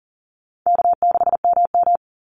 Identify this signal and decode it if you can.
Morse code, heard as a keyed tone, about 40 wpm, 700 Hz. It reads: K6MM
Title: Call Signs Repeated 3x - 40wpm